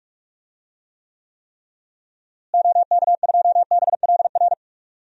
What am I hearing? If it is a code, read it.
Morse code, heard as a keyed tone, about 46 wpm, 700 Hz. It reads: OK2BLR
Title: DXCC-100 Call Signs - 6 Characters - Repeated 3x - 45wpm